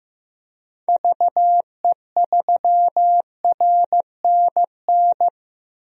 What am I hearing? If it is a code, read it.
Morse code, heard as a keyed tone, about 15 wpm, 700 Hz. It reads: VE3RNN
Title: DXCC-100 Call Signs - 6 Characters - Repeated 2x - 15wpm